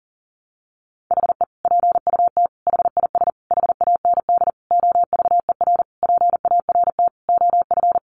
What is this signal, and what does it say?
Morse code, heard as a keyed tone, about 40 wpm, 700 Hz. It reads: HE PUT HIS HAND OVER PART OF
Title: Sentences from Top 400 Words 40wpm